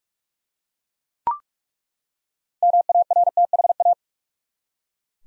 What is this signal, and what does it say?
Morse code, heard as a keyed tone, about 42 wpm, 700 Hz. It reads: MARTHA
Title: USA Common Names - 45wpm